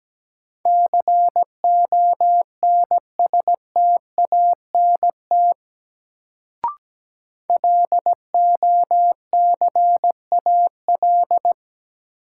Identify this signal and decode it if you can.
Morse code, heard as a keyed tone, about 17 wpm, 700 Hz. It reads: CONSTANT LOCAL
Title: Sentences from Top 1500 Words - New Word Drill Alt - 17wpm